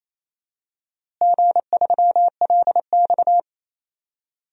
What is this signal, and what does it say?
Morse code, heard as a keyed tone, about 28 wpm, 700 Hz. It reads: G3LX
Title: DXCC-100 Call Signs - 4 Characters - Repeated 3x - 28wpm